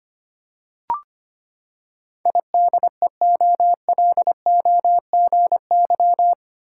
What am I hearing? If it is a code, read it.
Morse code, heard as a keyed tone, about 25 wpm, 700 Hz. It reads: IDEOLOGY
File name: Eight Letter Words - 25wpm